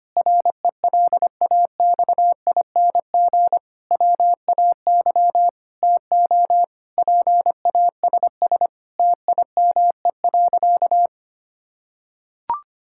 Morse code, heard as a keyed tone, about 25 wpm, 700 Hz. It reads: RELAXING WAY TO PASS TIME.
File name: practice / 25wpm / Difficult Sentences with Hints - 25wpm